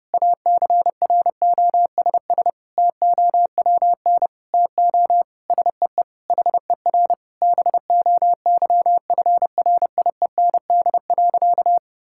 Morse code, heard as a keyed tone, about 30 wpm, 700 Hz. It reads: ACROSS TOWN TO SEE HER BOYFRIEND.
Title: Sentences from Top 1000 Words - Review Part 6 - 30wpm